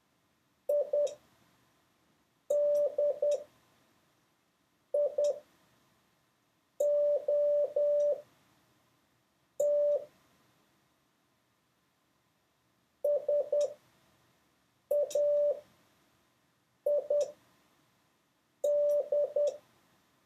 Morse code, heard as a keyed tone, about 10 wpm, spaced as for 5 wpm, 580 Hz. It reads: IDIOT SAID
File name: 6of6-test